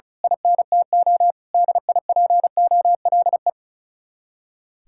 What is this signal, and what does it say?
Morse code, heard as a keyed tone, about 33 wpm, 700 Hz. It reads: INTO DIPOLE
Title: Speed-Racing Common QSO Phrases Encore - 35wpm